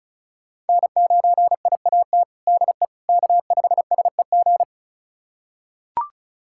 Morse code, heard as a keyed tone, about 34 wpm, 700 Hz. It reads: N9IAT DE K5SEG
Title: USA Callsign DE Callsign - 5 Characters - 35wpm